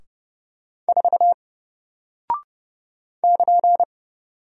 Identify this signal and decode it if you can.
Morse code, heard as a keyed tone, about 30 wpm, 700 Hz. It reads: 4 <KN>
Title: Single Character (Letter:Number:Punctuation:Prosign) Alt 30wpm